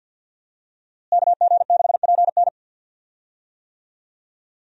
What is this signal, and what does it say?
Morse code, heard as a keyed tone, about 49 wpm, 700 Hz. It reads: KG6PN
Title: USA Call Signs - 5 Characters - Repeated 3x - 50wpm